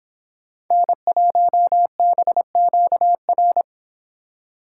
Morse code, heard as a keyed tone, about 27 wpm, 700 Hz. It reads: N1BQR
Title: Speed Racing - USA Call Signs 22wpm